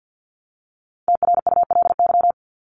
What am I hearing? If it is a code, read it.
Morse code, heard as a keyed tone, about 50 wpm, 700 Hz. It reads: TRULY
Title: Top 1501-1750 Words 50wpm